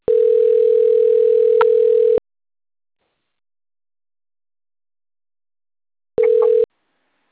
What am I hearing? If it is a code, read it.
Ringback tone (North America), 440 + 480 Hz.
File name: out- - - - - .10077